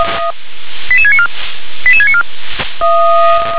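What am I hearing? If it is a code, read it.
Five-tone selective call (ZVEI), heard as quick runs of five tones, tones of about 70 ms. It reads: 88694 88694
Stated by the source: ZVEI five-tone call 88747 received on an SDR.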